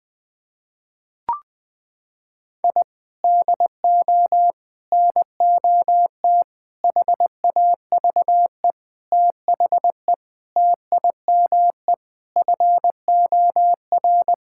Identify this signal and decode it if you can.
Morse code, heard as a keyed tone, about 20 wpm, 700 Hz. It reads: I DO NOT HAVE THE TIME FOR
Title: Sentences with 5-Word Phrases - 20wpm